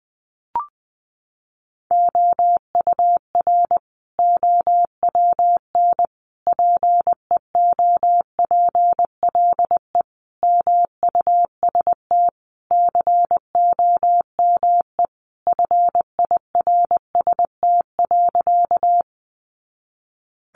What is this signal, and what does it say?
Morse code, heard as a keyed tone, about 20 wpm, 700 Hz. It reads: OUR OWN PEOPLE MUST COME FIRST.
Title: Sentences from Top 300 Words 20wpm